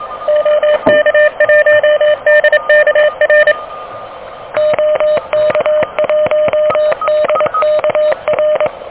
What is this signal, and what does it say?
Morse code, heard as a keyed tone, about 23 wpm, 610 Hz. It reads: OK1DKR OK1DKR
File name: OK QRP 2023 / OK QRP 2023